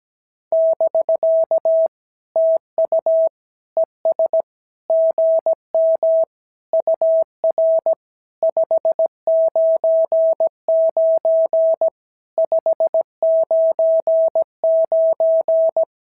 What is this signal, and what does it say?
Morse code, heard as a keyed tone, about 17 wpm, 650 Hz. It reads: <BK> TU ES GM UR 599 599
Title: SOTA QSO Practice 17wpm